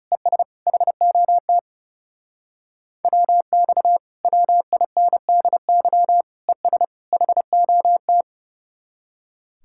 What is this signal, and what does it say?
Morse code, heard as a keyed tone, about 39 wpm, 700 Hz. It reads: ES HOT WX WINDY ES HOT